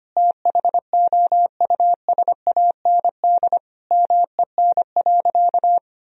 Morse code, heard as a keyed tone, about 25 wpm, 700 Hz. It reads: THOUSAND MEN.
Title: Sentences from Top 1000 Words - Review Part 3 - 25wpm